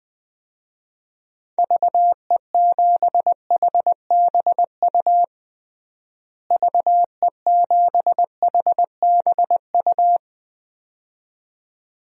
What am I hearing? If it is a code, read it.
Morse code, heard as a keyed tone, about 20 wpm, 700 Hz. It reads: VE7HBU VE7HBU
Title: DXCC-100 Call Signs - 6 Characters - Repeated 2x - 20wpm